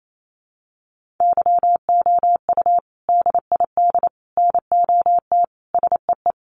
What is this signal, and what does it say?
Morse code, heard as a keyed tone, about 28 wpm, 700 Hz. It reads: YOU DID NOT SEE
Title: Sentences from Top 900 Words 28wpm